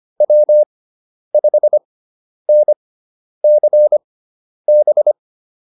Morse code, heard as a keyed tone, about 25 wpm, spaced as for 10 wpm, 600 Hz. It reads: W5NCB